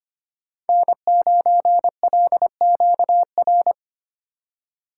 Morse code, heard as a keyed tone, about 25 wpm, 700 Hz. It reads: N9LQR
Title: USA Call Signs - 5 Characters - 3x Character Spacing - 25wpm